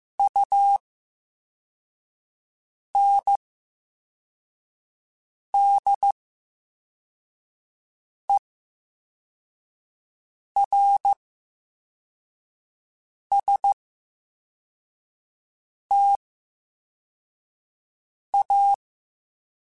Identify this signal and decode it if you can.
Morse code, heard as a keyed tone, about 15 wpm, spaced as for 4 wpm, 800 Hz. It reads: UNDERSTA
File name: Special INGs-mp3s / Special INGs-mp3s / VerySlowToFaster Ing @13